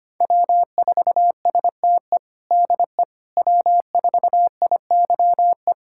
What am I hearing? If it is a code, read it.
Morse code, heard as a keyed tone, about 25 wpm, 700 Hz. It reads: W4STE DE W4IYE
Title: USA Callsign DE Callsign - 5 Characters - 25wpm